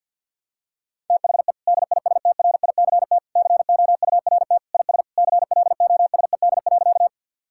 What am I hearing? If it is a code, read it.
Morse code, heard as a keyed tone, about 50 wpm, 700 Hz. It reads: THE DISTRICT COURT IS CLOSED.
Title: Sentences from Top 1600 Words 50wpm